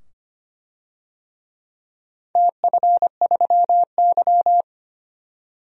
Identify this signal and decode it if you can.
Morse code, heard as a keyed tone, about 25 wpm, 700 Hz. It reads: TF3Y
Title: DXCC-100 Call Signs - 4 Characters - 25wpm